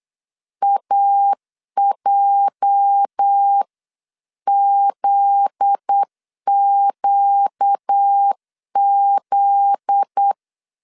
Morse code, heard as a keyed tone, about 8 wpm, 800 Hz. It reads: AJZQZ